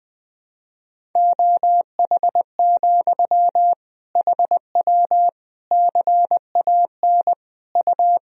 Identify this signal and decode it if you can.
Morse code, heard as a keyed tone, about 20 wpm, 700 Hz. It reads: OH, HW CAN U
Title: Sentences from Top 500 Words and Abbreviations 20wpm